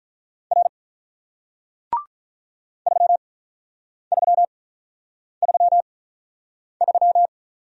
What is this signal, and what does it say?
Morse code, heard as a keyed tone, about 48 wpm, spaced as for 9 wpm, 700 Hz. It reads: R 3333